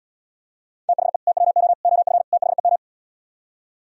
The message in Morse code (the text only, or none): THE TWO OF THEM